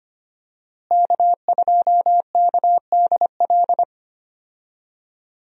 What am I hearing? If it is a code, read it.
Morse code, heard as a keyed tone, about 25 wpm, 700 Hz. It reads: K2KDL
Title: USA Call Signs - 5 Characters - Repeated 2x - 25wpm